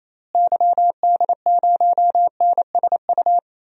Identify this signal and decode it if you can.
Morse code, heard as a keyed tone, about 28 wpm, 700 Hz. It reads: YD0NSU
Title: DXCC-100 Call Signs - 6 Characters - Repeated 3x - 28wpm